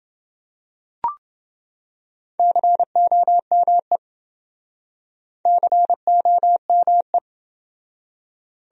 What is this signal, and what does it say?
Morse code, heard as a keyed tone, about 27 wpm, 700 Hz. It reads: COME COME